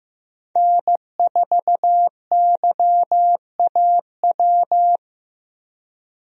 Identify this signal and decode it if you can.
Morse code, heard as a keyed tone, about 15 wpm, 700 Hz. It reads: N4YAW
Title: USA Call Signs - 5 Characters - Repeated 3x - 15wpm